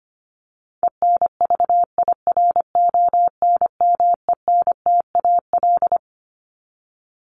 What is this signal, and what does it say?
Morse code, heard as a keyed tone, about 25 wpm, 700 Hz. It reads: ENVIRONMENTAL
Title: Top 1001-1250 Words 25wpm